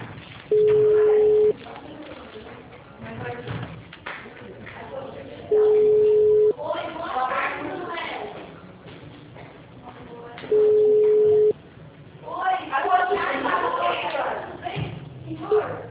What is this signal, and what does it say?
Ringback tone (Europe, 425 Hz), with speech.